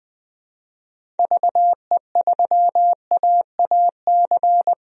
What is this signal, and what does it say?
Morse code, heard as a keyed tone, about 20 wpm, 700 Hz. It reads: VE3AAC
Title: DXCC-100 Call Signs - 6 Characters - Repeated 3x - 20wpm